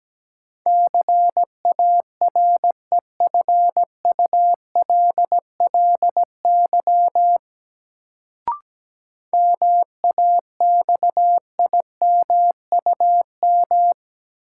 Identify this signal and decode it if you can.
Morse code, heard as a keyed tone, about 17 wpm, 700 Hz. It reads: CAREFULLY MAXIMUM
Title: Sentences from Top 1300 Words - New Word Drill Alt - 17wpm